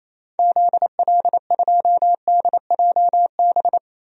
Morse code, heard as a keyed tone, about 28 wpm, 700 Hz. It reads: ZL2DJB
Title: DXCC-100 Call Signs - 6 Characters - 28wpm